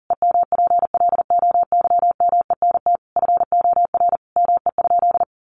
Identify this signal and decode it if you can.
Morse code, heard as a keyed tone, about 40 wpm, 700 Hz. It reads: EMPLOYMENT FOR ME?
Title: Sentences from Top 1500 Words 40wpm